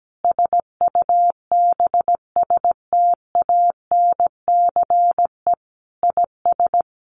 Morse code, heard as a keyed tone, about 17 wpm, 700 Hz. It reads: SUBSTANCE IS
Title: Sentences from Top 1500 Words 17wpm